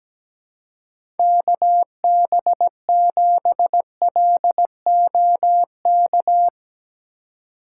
Morse code, heard as a keyed tone, about 17 wpm, 700 Hz. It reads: KB7LOK
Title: USA Call Signs - 6 Characters - 4x Character Spacing - 17wpm